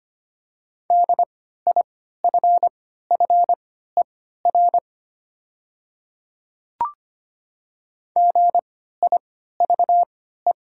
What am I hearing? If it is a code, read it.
Morse code, heard as a keyed tone, about 25 wpm, spaced as for 14 wpm, 700 Hz. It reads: DIFFER GIVE
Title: 3x Character Spacing - Top 300 Words - 25wpm